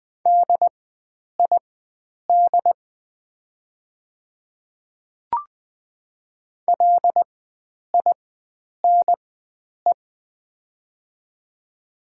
Morse code, heard as a keyed tone, about 20 wpm, spaced as for 9 wpm, 700 Hz. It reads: DID LINE
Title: 4x Character Spacing - Top 300 Words - 20wpm